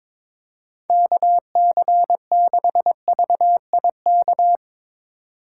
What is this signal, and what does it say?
Morse code, heard as a keyed tone, about 24 wpm, 700 Hz. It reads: KC6VIK
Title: Speed Racing - USA 2x3 Call Signs 22wpm